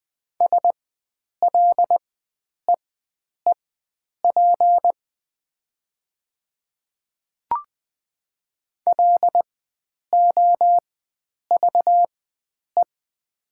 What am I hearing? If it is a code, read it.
Morse code, heard as a keyed tone, about 20 wpm, spaced as for 9 wpm, 700 Hz. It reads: SLEEP LOVE
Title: 4x Character Spacing - Top 500 Words - 20wpm